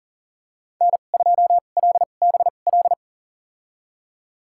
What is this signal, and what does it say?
Morse code, heard as a keyed tone, about 40 wpm, spaced as for 29 wpm, 700 Hz. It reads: N2LBL
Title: USA Call Signs - 2x Character Spacing - 40wpm